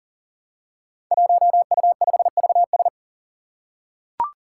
Morse code, heard as a keyed tone, about 40 wpm, 700 Hz. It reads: 1U5VS
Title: Five Challenging Letter-Number Combinations 40wpm